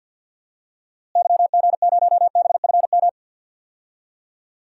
Morse code, heard as a keyed tone, about 50 wpm, 700 Hz. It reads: YG0BFM
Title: DXCC-100 Call Signs - 6 Characters - Repeated 2x - 50wpm